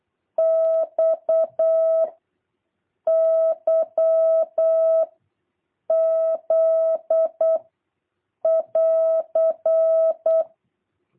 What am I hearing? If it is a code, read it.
Morse code, heard as a keyed tone, about 8 wpm, 650 Hz. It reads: X Y Z <AR>
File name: jh8cbh / 2023cw